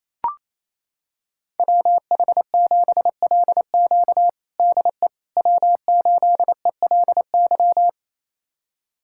Morse code, heard as a keyed tone, about 28 wpm, 700 Hz. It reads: WH7LQ DE W8ELY
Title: USA Callsign DE Callsign - 5 Characters - 28wpm